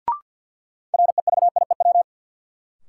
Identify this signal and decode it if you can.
Morse code, heard as a keyed tone, about 47 wpm, 700 Hz. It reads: REVIEW
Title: Sentences from Top 900 Words - New Word Drill - 50wpm